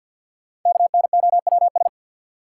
Morse code, heard as a keyed tone, about 52 wpm, 700 Hz. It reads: KNOWS
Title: Sentences from Top 1000 Words - Review Part 1 - New Word Drill - 50wpm